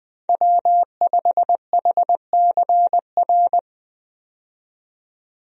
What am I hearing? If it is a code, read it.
Morse code, heard as a keyed tone, about 20 wpm, 700 Hz. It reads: W5HCR
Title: USA Call Signs - 5 Characters - 20wpm